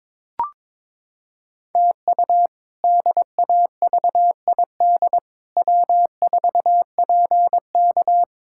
Morse code, heard as a keyed tone, about 22 wpm, 700 Hz. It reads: TU DAVID W4PK
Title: K1USN SST QSO Practice - 22wpm